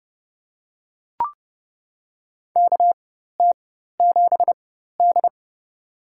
Morse code, heard as a keyed tone, about 30 wpm, spaced as for 14 wpm, 700 Hz. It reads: KT7D
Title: USA Call Signs - 4 Characters - 4x Character Spacing - 30wpm